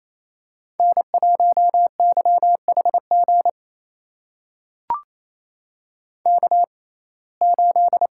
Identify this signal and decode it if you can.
Morse code, heard as a keyed tone, about 28 wpm, spaced as for 20 wpm, 700 Hz. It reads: N1YHG K8
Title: USA Call Signs - 5 Characters - 6x Character Spacing - 28wpm